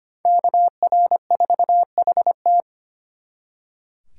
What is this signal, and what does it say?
Morse code, heard as a keyed tone, about 32 wpm, 700 Hz. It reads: KR4HT